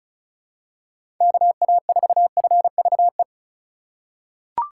Morse code, heard as a keyed tone, about 36 wpm, 700 Hz. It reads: KA4FVE
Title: USA Call Signs - 6 Characters - 2x Character Spacing - 35wpm